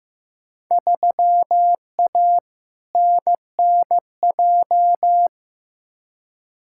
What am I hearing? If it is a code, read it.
Morse code, heard as a keyed tone, about 15 wpm, 700 Hz. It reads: 3A NNJ
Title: Field Day Exchanges - 15wpm